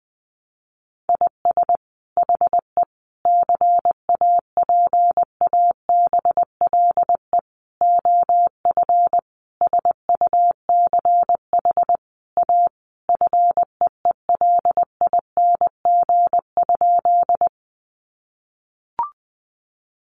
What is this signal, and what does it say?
Morse code, heard as a keyed tone, about 20 wpm, 700 Hz. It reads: IS HE CAPABLE OF SUCH A FEELING?
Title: Sentences from Top 1200 Words 20wpm